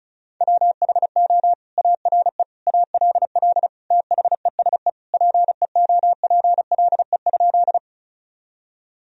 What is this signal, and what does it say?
Morse code, heard as a keyed tone, about 34 wpm, 700 Hz. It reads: WHO ARE ALL THESE PEOPLE?
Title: Sentences from Top 1000 Words - Review Part 1 - 35wpm